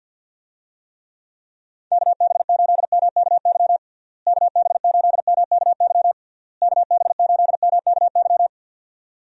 Morse code, heard as a keyed tone, about 52 wpm, 700 Hz. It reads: KB8MKY KB8MKY KB8MKY
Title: USA Call Signs - 6 Characters - Repeated 3x - 50wpm